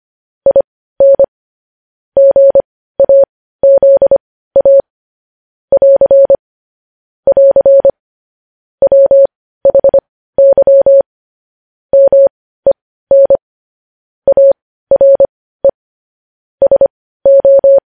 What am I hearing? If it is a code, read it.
Morse code, heard as a keyed tone, about 25 wpm, spaced as for 15 wpm, 550 Hz. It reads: IN GAZA <AR> <AR> WHY MEN ARE SO
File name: podcast / News headlines-1x-25x15